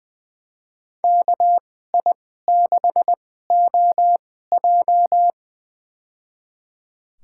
Morse code, heard as a keyed tone, about 21 wpm, spaced as for 15 wpm, 700 Hz. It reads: KI6OJ
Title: USA Call Signs - 5 Characters - 2x Character Spacing - 20wpm